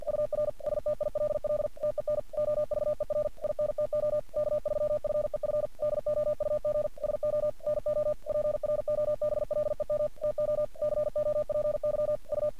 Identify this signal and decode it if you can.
Morse code, heard as a keyed tone, about 46 wpm, 600 Hz. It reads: UR STILL TEN OVER INTO K3FEF DOUG SO NO PROBLEM TO COPY U
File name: archived / CRSnet